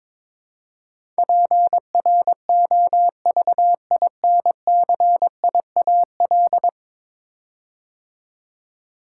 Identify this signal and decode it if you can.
Morse code, heard as a keyed tone, about 22 wpm, 700 Hz. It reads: PROVINCIAL
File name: practice / 22wpm / 10 Letter Words - First 3 Letters Given - 22wpm